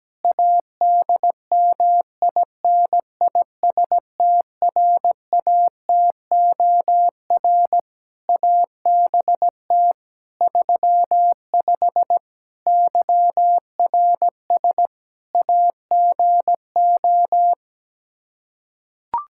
Morse code, heard as a keyed tone, about 17 wpm, 700 Hz. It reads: ADMINISTRATOR ABT 35 YRS AGO
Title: Practice QSO - Exchange 3 - More Info - 17wpm